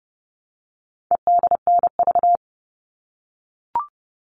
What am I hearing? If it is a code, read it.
Morse code, heard as a keyed tone, about 30 wpm, 700 Hz. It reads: EDNV
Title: All Cut Numbers - 4-Digit 30wpm